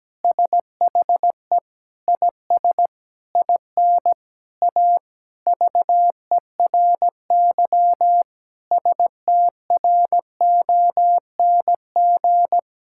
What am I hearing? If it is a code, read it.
Morse code, heard as a keyed tone, about 17 wpm, 700 Hz. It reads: SHE IS IN A VERY STRONG
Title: Sentences from Top 1000 Words - Review Part 5 - 17wpm